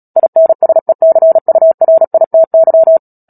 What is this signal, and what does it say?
Morse code, heard as a keyed tone, about 37 wpm, 650 Hz. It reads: INSECURITY
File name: Book Two / 10-Chapter 10 - Epilogue - 36 WPM